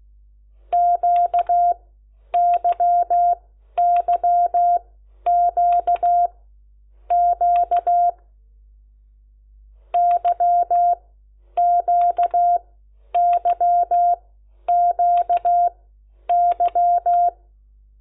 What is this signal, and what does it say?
Morse code, heard as a keyed tone, about 16 wpm, spaced as for 10 wpm, 700 Hz. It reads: QYYQQ YQYQY
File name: Lecciones / CLASE Nº10